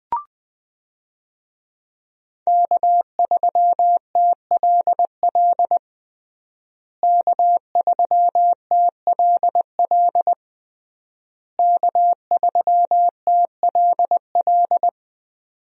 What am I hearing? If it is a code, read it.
Morse code, heard as a keyed tone, about 20 wpm, 700 Hz. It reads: K3TLL K3TLL K3TLL
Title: USA Call Signs - 5 Characters - Repeated 3x - 20wpm